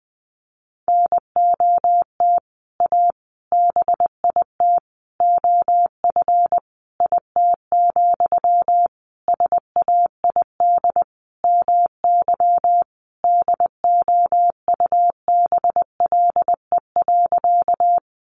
Morse code, heard as a keyed tone, about 19 wpm, 700 Hz. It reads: NOT A BIT OF IT, SAID MY DOUBLE.
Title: Sentences from Top 1400 Words 20wpm